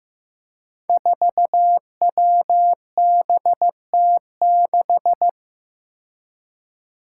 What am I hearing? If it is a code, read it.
Morse code, heard as a keyed tone, about 15 wpm, 700 Hz. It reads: 4WBT6
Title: Five Challenging Letter-Number Combinations 15wpm